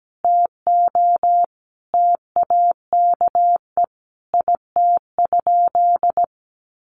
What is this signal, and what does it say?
Morse code, heard as a keyed tone, about 17 wpm, 700 Hz. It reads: TO TAKE IT?
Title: Sentences from Top 300 Words 17wpm